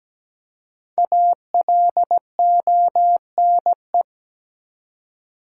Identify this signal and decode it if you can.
Morse code, heard as a keyed tone, about 17 wpm, 700 Hz. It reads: ALONE
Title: Five Letter Words - 17wpm